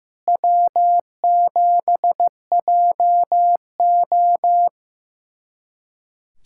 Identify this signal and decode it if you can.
Morse code, heard as a keyed tone, about 15 wpm, 700 Hz. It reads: W7JO
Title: USA Callsign DE Callsign - 4 Characters - 15wpm